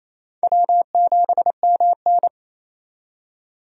Morse code, heard as a keyed tone, about 28 wpm, 700 Hz. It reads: W7MN
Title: USA Call Signs - 4 Characters - 28wpm